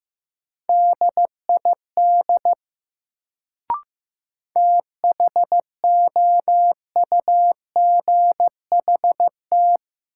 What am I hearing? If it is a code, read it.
Morse code, heard as a keyed tone, about 15 wpm, 700 Hz. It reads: DID THOUGHT
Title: Sets of 2 Words - New Word Drill - 15wpm